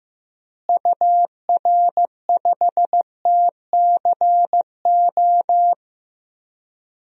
Morse code, heard as a keyed tone, about 15 wpm, 700 Hz. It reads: UR5TCO
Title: DXCC-100 Call Signs - 6 Characters - Repeated 2x - 15wpm